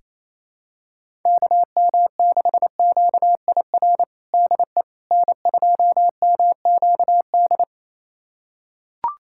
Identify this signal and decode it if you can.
Morse code, heard as a keyed tone, about 28 wpm, 700 Hz. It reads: KM6QIR DE N2MQD
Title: USA Callsign DE Callsign 28wpm